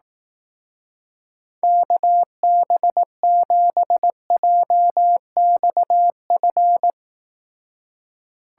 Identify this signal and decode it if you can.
Morse code, heard as a keyed tone, about 18 wpm, 700 Hz. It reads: KB7JXF
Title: Speed Racing - USA Call Signs 15wpm